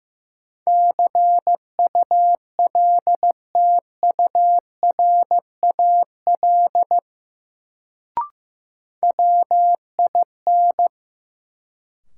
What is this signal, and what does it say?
Morse code, heard as a keyed tone, about 16 wpm, 700 Hz. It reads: CULTURAL WIN
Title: Sentences from Top 1500 Words - New Word Drill Alt - 15wpm